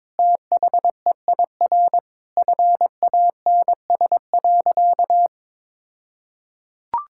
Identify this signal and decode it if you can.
Morse code, heard as a keyed tone, about 22 wpm, 700 Hz. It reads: THEIR FANS.
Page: morse-code-ninja-practice-22wpm